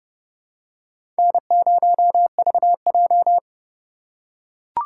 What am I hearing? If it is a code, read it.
Morse code, heard as a keyed tone, about 32 wpm, 700 Hz. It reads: N0VJ
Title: USA Call Signs - 2x Character Spacing - 30wpm